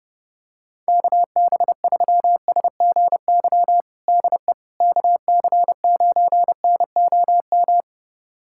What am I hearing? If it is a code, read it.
Morse code, heard as a keyed tone, about 30 wpm, 700 Hz. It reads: KB3SGY DE KC9NOM
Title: USA Callsign DE Callsign - 6 Characters - 30wpm